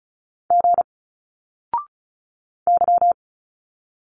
Morse code, heard as a keyed tone, about 35 wpm, 700 Hz. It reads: G Y
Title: Single Letters 35wpm